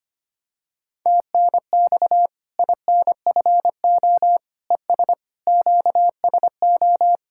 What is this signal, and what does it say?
Morse code, heard as a keyed tone, about 25 wpm, 700 Hz. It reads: TNX INFO ES QSO
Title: Practice QSO - Exchange 4 - Ending - 25wpm